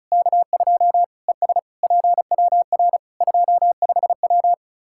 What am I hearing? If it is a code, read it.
Morse code, heard as a keyed tone, about 34 wpm, 700 Hz. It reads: K2 ES PWR 25W
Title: Practice QSO - Exchange 2 - Info - 35wpm